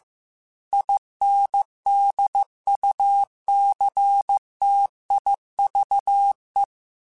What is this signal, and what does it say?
Morse code, heard as a keyed tone, about 15 wpm, 800 Hz. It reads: INDUCTIVE
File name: Technical Subjects / Technical Subjects-mp3s / XL @13